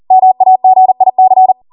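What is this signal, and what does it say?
Morse code, heard as a keyed tone, about 40 wpm, 750 Hz. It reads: MAGIC
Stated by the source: Code practice at 40 wpm.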